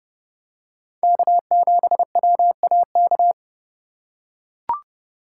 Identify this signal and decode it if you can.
Morse code, heard as a keyed tone, about 32 wpm, 700 Hz. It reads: K7WAK
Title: USA Call Signs - 3x Character Spacing - 30wpm